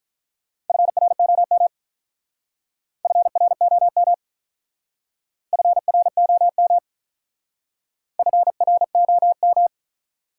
Morse code, heard as a keyed tone, about 45 wpm, 700 Hz. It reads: FROM FROM FROM FROM